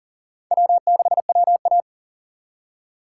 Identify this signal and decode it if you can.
Morse code, heard as a keyed tone, about 51 wpm, 700 Hz. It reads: W6WA